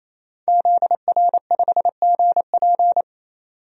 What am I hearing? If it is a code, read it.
Morse code, heard as a keyed tone, about 27 wpm, 700 Hz. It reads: ZR5GP
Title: DXCC-100 Call Signs - 5 Characters - 28wpm